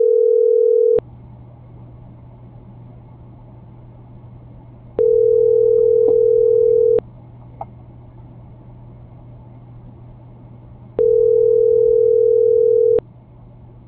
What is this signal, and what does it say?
Ringback tone (North America), 440 + 480 Hz.